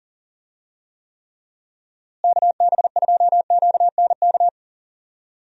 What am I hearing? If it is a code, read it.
Morse code, heard as a keyed tone, about 39 wpm, 700 Hz. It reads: KB2QNK
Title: USA Call Signs - 6 Characters - Repeated 3x - 40wpm